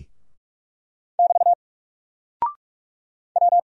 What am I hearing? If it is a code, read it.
Morse code, heard as a keyed tone, about 44 wpm, 700 Hz. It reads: <BT> W
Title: Single Character (Letter:Number:Punctuation:Prosign) Alt 45wpm